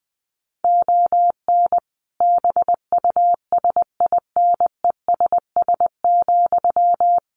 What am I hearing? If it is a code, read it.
Morse code, heard as a keyed tone, about 20 wpm, 700 Hz. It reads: ON BUSINESS,
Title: Sentences from Top 1300 Words 20wpm